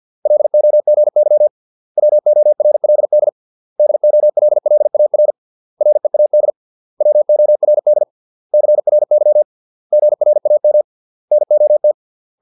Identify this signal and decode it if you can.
Morse code, heard as a keyed tone, about 50 wpm, 600 Hz. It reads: LOGY WORLD DOLLAR READ WORD CRY GRAM NOT